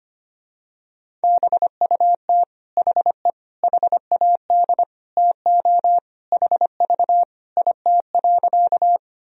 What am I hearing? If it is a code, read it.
Morse code, heard as a keyed tone, about 25 wpm, 700 Hz. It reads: BUT HE HAD TO HV IT.
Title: Sentences from Top 100 Words and Abbreviations 25wpm